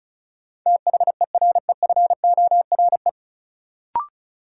Morse code, heard as a keyed tone, about 36 wpm, 700 Hz. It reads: THEREFORE
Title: Sentences from Top 800 Words - New Word Drill - 35wpm